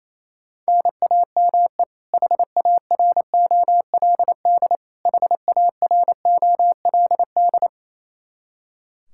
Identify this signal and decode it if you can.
Morse code, heard as a keyed tone, about 28 wpm, 700 Hz. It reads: NAME HAROLD HAROLD
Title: Practice QSO - Exchange 1 - Beginning - 28wpm